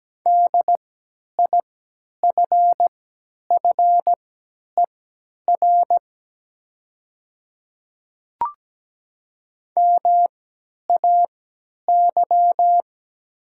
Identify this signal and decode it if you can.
Morse code, heard as a keyed tone, about 17 wpm, spaced as for 10 wpm, 700 Hz. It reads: DIFFER MAY
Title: 3x Character Spacing - Top 300 Words - 17wpm